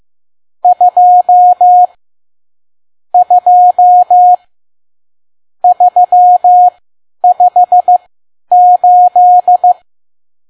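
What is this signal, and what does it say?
Morse code, heard as a keyed tone, about 15 wpm, spaced as for 10 wpm, 720 Hz. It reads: 2 2 358
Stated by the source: Morse code practice at 10 WPM.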